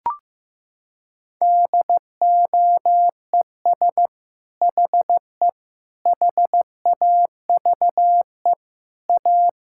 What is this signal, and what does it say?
Morse code, heard as a keyed tone, about 15 wpm, 700 Hz. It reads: DOES HE HAVE A
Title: Sentences from Top 1600 Words 15wpm